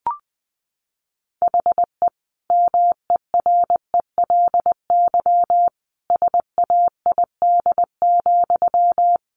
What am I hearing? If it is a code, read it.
Morse code, heard as a keyed tone, about 19 wpm, 700 Hz. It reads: HE MERELY SAID,
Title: Sentences from Top 1400 Words 20wpm